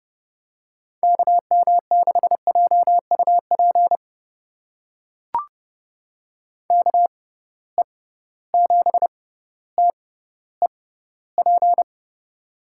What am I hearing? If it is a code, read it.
Morse code, heard as a keyed tone, about 31 wpm, spaced as for 15 wpm, 700 Hz. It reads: KM6JUP KE7TEP
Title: USA Call Signs - 6 Characters - 6x Character Spacing - 30wpm